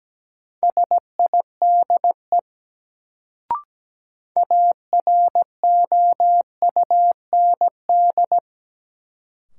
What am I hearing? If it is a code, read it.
Morse code, heard as a keyed tone, about 17 wpm, 700 Hz. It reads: SIDE AROUND
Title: Sets of 2 Words - New Word Drill - 17wpm